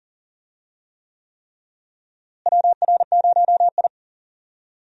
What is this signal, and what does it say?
Morse code, heard as a keyed tone, about 40 wpm, 700 Hz. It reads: WR0I
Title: USA Call Signs - 4 Characters - Repeated 3x - 40wpm